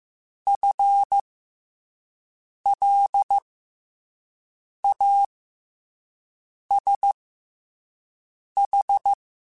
Morse code, heard as a keyed tone, about 15 wpm, spaced as for 5 wpm, 800 Hz. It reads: FLASH
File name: FL Runs / FL Runs mp3s / FL Runs @5